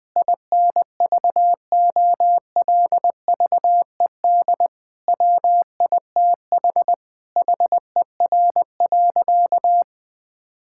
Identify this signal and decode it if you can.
Morse code, heard as a keyed tone, about 20 wpm, 700 Hz. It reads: INVOLVED WITH HER.
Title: Sentences from Top 1000 Words - Review Part 5 - 20wpm